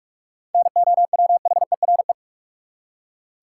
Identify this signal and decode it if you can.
Morse code, heard as a keyed tone, about 46 wpm, 700 Hz. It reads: NOWHERE
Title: Seven Letter Words - 45wpm